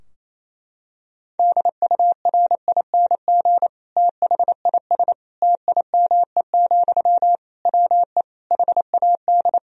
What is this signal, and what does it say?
Morse code, heard as a keyed tone, about 28 wpm, 700 Hz. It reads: DURING THIS TIME, WE HAD
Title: Sentences from Top 1000 Words - Review Part 2 - 28wpm